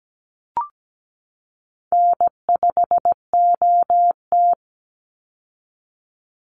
Morse code, heard as a keyed tone, about 17 wpm, 700 Hz. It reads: N5OT
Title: Easy Call Signs - 17wpm